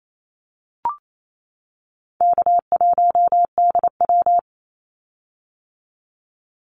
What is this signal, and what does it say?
Morse code, heard as a keyed tone, about 28 wpm, 700 Hz. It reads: K1DW
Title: Easy Call Signs - 28wpm